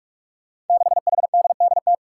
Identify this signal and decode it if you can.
Morse code, heard as a keyed tone, about 44 wpm, 700 Hz. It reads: 6HDDT
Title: Five Challenging Letter-Number Combinations 45wpm